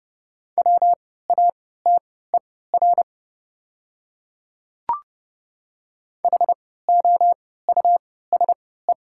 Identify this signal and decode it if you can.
Morse code, heard as a keyed tone, about 30 wpm, spaced as for 17 wpm, 700 Hz. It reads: WATER HOUSE
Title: 3x Character Spacing - Top 300 Words - 30wpm